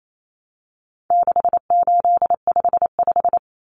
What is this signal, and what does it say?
Morse code, heard as a keyed tone, about 28 wpm, 700 Hz. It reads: 6855
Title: Common Cut Numbers - 4-Digit 28wpm